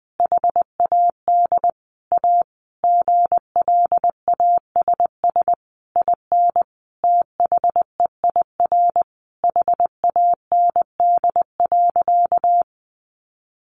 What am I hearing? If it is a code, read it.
Morse code, heard as a keyed tone, about 20 wpm, 700 Hz. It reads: HAD A GLASS IN THEIR HAND.
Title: Sentences from Top 1300 Words 20wpm